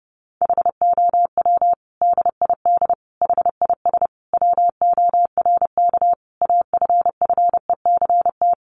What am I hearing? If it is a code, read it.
Morse code, heard as a keyed tone, about 30 wpm, 700 Hz. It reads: HOW DID HIS WORK AFFECT
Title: Sentences from Top 900 Words 30wpm